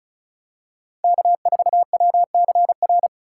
Morse code, heard as a keyed tone, about 36 wpm, 700 Hz. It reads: K4WCR
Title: USA Call Signs - 5 Characters - 3x Character Spacing - 35wpm